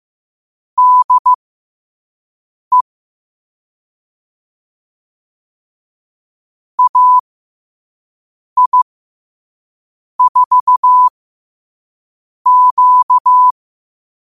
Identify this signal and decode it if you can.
Morse code, heard as a keyed tone, about 15 wpm, spaced as for 5 wpm, 1000 Hz. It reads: DE AI4Q
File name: qotdfiles / 5wpm